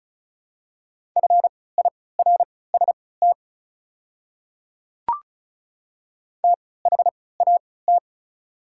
Morse code, heard as a keyed tone, about 36 wpm, spaced as for 20 wpm, 690 Hz. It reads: FIRST THAT
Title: 3x Character Spacing - Top 200 Words - 35wpm